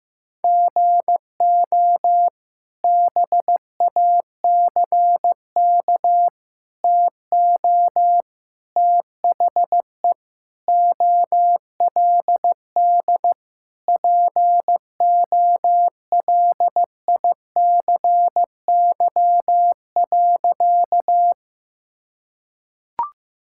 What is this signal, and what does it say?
Morse code, heard as a keyed tone, about 15 wpm, 700 Hz. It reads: GO BACK TO THE OLD POLICY.
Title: Sentences from Top 1000 Words - Review Part 5 - 15wpm